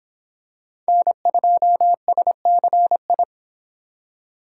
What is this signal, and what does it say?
Morse code, heard as a keyed tone, about 25 wpm, 700 Hz. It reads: N2SCI